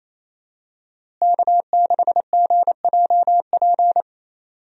Morse code, heard as a keyed tone, about 28 wpm, 700 Hz. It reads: K6GJP
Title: USA Call Signs - 5 Characters - Repeated 2x - 28wpm